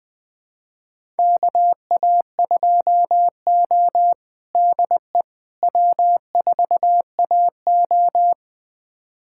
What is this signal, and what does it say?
Morse code, heard as a keyed tone, about 20 wpm, 700 Hz. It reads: KA2O DE W4AO
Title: USA Callsign DE Callsign - 4 Characters - 20wpm